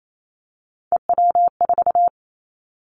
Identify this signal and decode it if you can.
Morse code, heard as a keyed tone, about 28 wpm, 700 Hz. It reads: EW4
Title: Callsign Prefixes for Top 100 DXCC - 28wpm